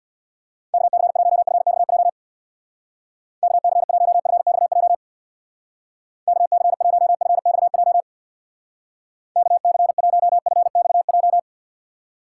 Morse code, heard as a keyed tone, about 57 wpm, 700 Hz. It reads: KC1FXJ KC1FXJ KC1FXJ KC1FXJ
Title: Speed Racing - USA 2x3 Call Signs 50wpm